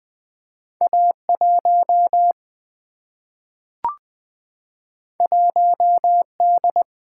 Morse code, heard as a keyed tone, about 20 wpm, 700 Hz. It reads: A1 1D
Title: Single-Letter Single-Number 20wpm